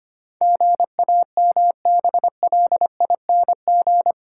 Morse code, heard as a keyed tone, about 25 wpm, 700 Hz. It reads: GAMBLING
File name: Eight Letter Words - 25wpm